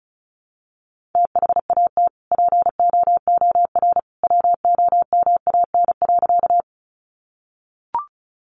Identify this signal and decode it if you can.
Morse code, heard as a keyed tone, about 34 wpm, 700 Hz. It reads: THAT POOR WOMAN.
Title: Sentences from Top 1100 Words 35wpm